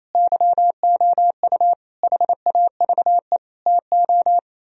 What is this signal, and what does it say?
Morse code, heard as a keyed tone, about 34 wpm, 700 Hz. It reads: YOU HAVE TO
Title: Speed-Racing - Sets of 5 Words 28wpm